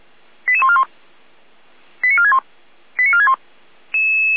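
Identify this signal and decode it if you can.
Five-tone selective call (ZVEI), heard as quick runs of five tones, tones of about 80 ms. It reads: 88141 89461 89461
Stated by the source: ZVEI five-tone call 88149 received by SDR, from the server of Feuerwehr Orlamünde (fire brigade).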